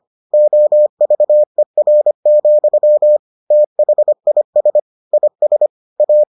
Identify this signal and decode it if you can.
Morse code, heard as a keyed tone, about 25 wpm, 600 Hz. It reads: OVER, THIS IS A